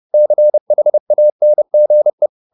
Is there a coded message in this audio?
Yes: Morse code from 0.1 to 2.3 s, heard as a keyed tone, about 30 wpm, 600 Hz. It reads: CHANGE